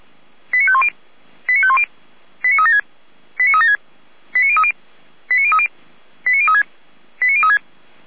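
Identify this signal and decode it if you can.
Five-tone selective call (ZVEI), heard as quick runs of five tones, tones of about 70 ms. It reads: 89510 89510 89376 89376 89030 89030 89036 89036
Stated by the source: Received on an SDR from server audio.f85360.feuerwehr-orlamuende.org (Feuerwehr Orlamünde fire brigade).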